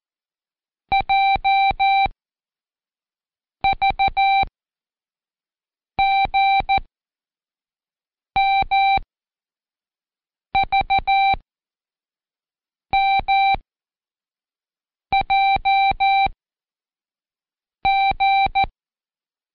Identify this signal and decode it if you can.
Morse code, heard as a keyed tone, about 14 wpm, spaced as for 5 wpm, 780 Hz. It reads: JVGMVMJG
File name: Code Quick / AudioDownld1 / Seg61-Day7-Intro V